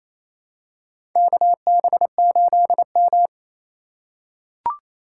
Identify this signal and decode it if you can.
Morse code, heard as a keyed tone, about 27 wpm, 700 Hz. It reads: KB8M
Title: K1USN SST QSO Practice - 28wpm